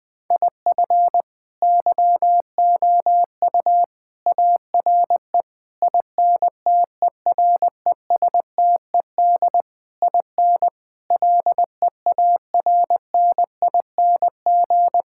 Morse code, heard as a keyed tone, about 20 wpm, 700 Hz. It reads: IF YOU ARE INTERESTED IN LEARNING
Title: Sentences with 8-Word Phrases - 20wpm